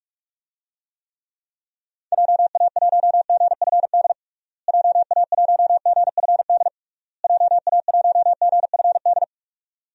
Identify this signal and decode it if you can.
Morse code, heard as a keyed tone, about 46 wpm, 700 Hz. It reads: JA1GFD JA1GFD JA1GFD
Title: DXCC-100 Call Signs - 6 Characters - Repeated 3x - 45wpm